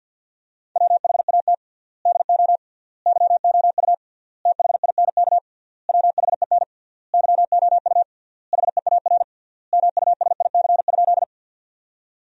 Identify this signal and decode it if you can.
Morse code, heard as a keyed tone, about 50 wpm, 700 Hz. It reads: WHAT DO YOU THINK WHEN YOU HEAR MUSIC?
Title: 3x Word-Spacing - Sentences from Top 300 Words - 50wpm